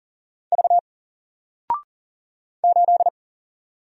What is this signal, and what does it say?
Morse code, heard as a keyed tone, about 40 wpm, 700 Hz. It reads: V 8
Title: Single Character (Letter:Number:Punctuation:Prosign) 40wpm